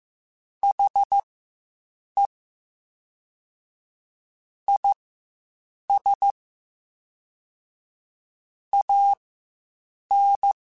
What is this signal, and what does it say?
Morse code, heard as a keyed tone, about 15 wpm, spaced as for 7 wpm, 800 Hz. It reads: HE IS AN
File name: Quotes Spaced / Quotes Spaced-mp3s / Optimist sp @10